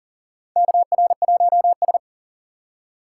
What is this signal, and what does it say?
Morse code, heard as a keyed tone, about 40 wpm, 700 Hz. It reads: KR1S
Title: USA Call Signs - 2x Character Spacing - 40wpm